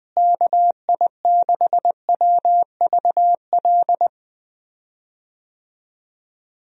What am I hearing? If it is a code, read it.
Morse code, heard as a keyed tone, about 20 wpm, 700 Hz. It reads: KI6WVL
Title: USA Call Signs - 6 Characters - Repeated 3x - 20wpm